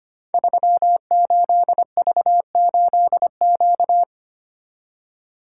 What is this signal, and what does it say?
Morse code, heard as a keyed tone, about 25 wpm, 700 Hz. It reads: 38V8Q
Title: Code Groups - 5 Letter:Numbers - 25wpm